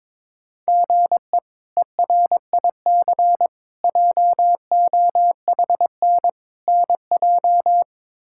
Morse code, heard as a keyed tone, about 22 wpm, 700 Hz. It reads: GE ERIC JOHN NJ
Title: K1USN SST QSO Practice - 22wpm